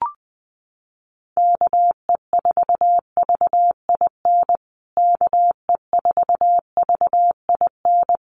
Morse code, heard as a keyed tone, about 20 wpm, 700 Hz. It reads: KE4VIN KE4VIN
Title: USA Call Signs Repeated 3x 20wpm